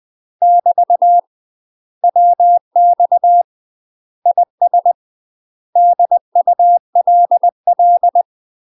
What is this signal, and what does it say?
Morse code, heard as a keyed tone, about 20 wpm, 700 Hz. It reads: <BT> WX IS DULL
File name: UK MASTER QSOs TRAINING / 20WPM / UK MASTER QSOs 0070